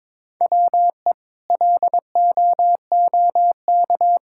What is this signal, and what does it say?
Morse code, heard as a keyed tone, about 22 wpm, 700 Hz. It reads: WE LOOK
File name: practice / 22wpm / Sets of 6 Words - 22wpm